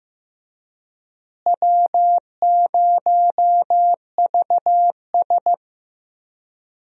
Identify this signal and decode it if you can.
Morse code, heard as a keyed tone, about 15 wpm, 700 Hz. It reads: W0VS
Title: USA Call Signs - 4 Characters - Repeated 2x - 15wpm